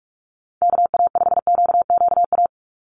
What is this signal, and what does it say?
Morse code, heard as a keyed tone, about 46 wpm, 700 Hz. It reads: KA5QQA
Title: NA SOTA Chaser Callsigns 45wpm